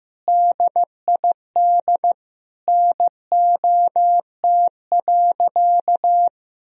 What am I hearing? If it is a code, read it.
Morse code, heard as a keyed tone, about 15 wpm, 700 Hz. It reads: DID NOT.
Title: Sentences from Top 1000 Words - Review Part 1 - 15wpm